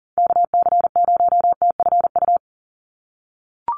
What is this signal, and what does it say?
Morse code, heard as a keyed tone, about 40 wpm, 700 Hz. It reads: KC0TFU
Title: USA Call Signs 40wpm